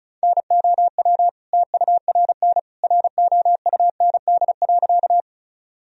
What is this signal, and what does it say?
Morse code, heard as a keyed tone, about 36 wpm, 700 Hz. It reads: NOW TURN ROUND.
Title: Sentences from Top 1000 Words - Review Part 1 - 35wpm